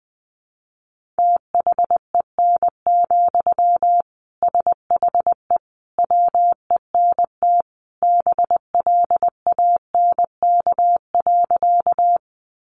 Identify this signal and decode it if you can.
Morse code, heard as a keyed tone, about 20 wpm, 700 Hz. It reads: THEN, SHE WENT BLANK.
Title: Sentences from Top 1500 Words 20wpm